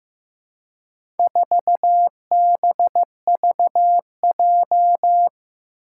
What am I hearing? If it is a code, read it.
Morse code, heard as a keyed tone, about 15 wpm, 700 Hz. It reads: 4BVJ
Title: Four Challenging Letter-Number Combinations 15wpm